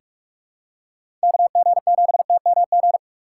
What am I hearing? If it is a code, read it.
Morse code, heard as a keyed tone, about 46 wpm, 700 Hz. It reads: KG7TMG
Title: USA Call Signs - 6 Characters - 2x Character Spacing - 45wpm